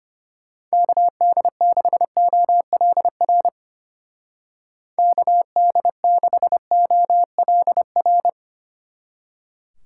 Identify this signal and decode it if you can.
Morse code, heard as a keyed tone, about 31 wpm, 700 Hz. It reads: KD6OLR KD6OLR
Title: Speed Racing - USA 2x3 Call Signs 25wpm